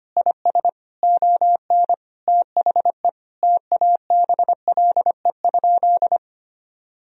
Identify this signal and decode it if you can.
Morse code, heard as a keyed tone, about 25 wpm, 700 Hz. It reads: IS ON THE TABLE?
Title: Sentences from Top 1000 Words - Review Part 4 - 25wpm